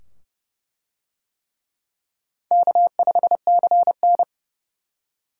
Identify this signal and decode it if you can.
Morse code, heard as a keyed tone, about 27 wpm, 700 Hz. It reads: K5CN